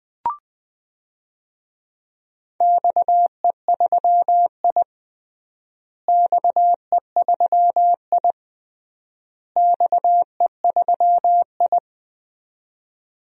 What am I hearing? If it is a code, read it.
Morse code, heard as a keyed tone, about 20 wpm, 700 Hz. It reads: XE3I XE3I XE3I
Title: DXCC-100 Call Signs - 4 Characters - Repeated 3x - 20wpm